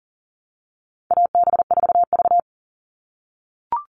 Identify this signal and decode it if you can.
Morse code, heard as a keyed tone, about 40 wpm, 700 Hz. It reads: AB4V
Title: All Cut Numbers - 4-Digit 40wpm